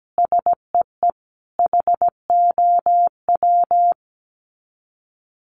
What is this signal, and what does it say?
Morse code, heard as a keyed tone, about 17 wpm, 700 Hz. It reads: SEE HOW
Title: Sets of 5 Words 17wpm